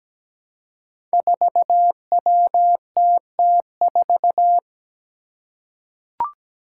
Morse code, heard as a keyed tone, about 17 wpm, 700 Hz. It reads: 4WTT4
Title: Five Challenging Letter-Number Combinations 17wpm